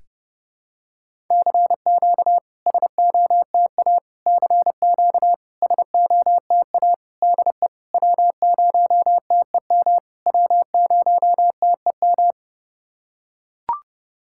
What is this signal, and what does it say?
Morse code, heard as a keyed tone, about 30 wpm, 690 Hz. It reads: CQ SOTA CQ SOTA DE W0TEM W0TEM
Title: SOTA QSO Practice 30wpm